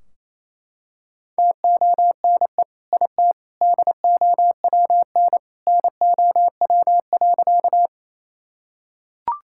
Morse code, heard as a keyed tone, about 28 wpm, 700 Hz. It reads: TONE IT DOWN NOW.
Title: Sentences from Top 1600 Words 28wpm